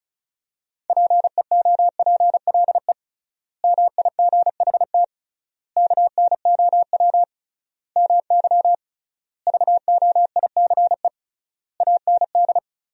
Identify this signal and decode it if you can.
Morse code, heard as a keyed tone, about 34 wpm, 700 Hz. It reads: PEOPLE MIGHT KNOW MY VOICE AND
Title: 3x Word-Spacing - Sentences from Top 500 Words - 35wpm